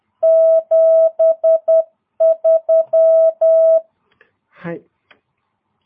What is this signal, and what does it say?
Morse code, heard as a keyed tone, about 10 wpm, 650 Hz. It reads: 73
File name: jh8cbh / 2022cw